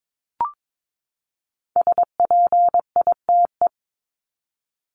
Morse code, heard as a keyed tone, about 22 wpm, 700 Hz. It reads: SPITE
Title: Top 1751-2000 Words 22wpm